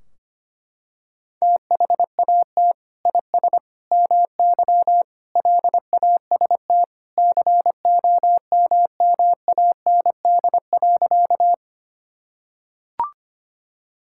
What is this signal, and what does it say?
Morse code, heard as a keyed tone, about 25 wpm, 700 Hz. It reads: THAT IS MY LAST COMMAND.
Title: Sentences from Top 1600 Words 25wpm